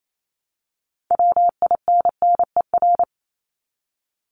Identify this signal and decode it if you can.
Morse code, heard as a keyed tone, about 28 wpm, 700 Hz. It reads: WINNER
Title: Top 1751-2000 Words 28wpm